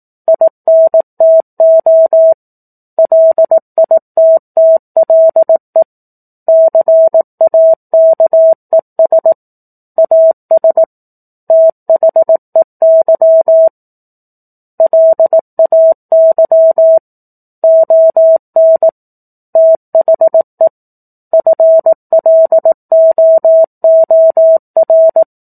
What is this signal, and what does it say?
Morse code, heard as a keyed tone, about 18 wpm, 650 Hz. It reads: INTO LITTLE CAKES AS THEY LAY ON THE FLOOR
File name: 04-Chapter-04 18 WPM